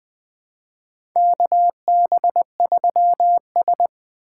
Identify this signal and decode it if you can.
Morse code, heard as a keyed tone, about 20 wpm, 700 Hz. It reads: KB3S
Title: USA Call Signs - 4 Characters - 2x Character Spacing - 20wpm